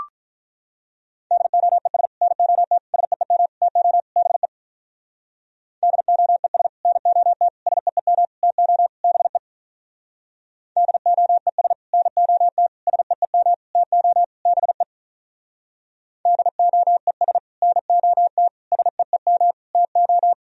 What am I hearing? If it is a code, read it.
Morse code, heard as a keyed tone, about 44 wpm, 700 Hz. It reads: DOES NOT SEEM TO BE DOES NOT SEEM TO BE DOES NOT SEEM TO BE DOES NOT SEEM TO